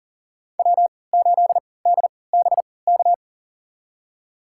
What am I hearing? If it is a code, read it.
Morse code, heard as a keyed tone, about 39 wpm, spaced as for 23 wpm, 700 Hz. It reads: W8DBK
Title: USA Call Signs - 5 Characters - 3x Character Spacing - 40wpm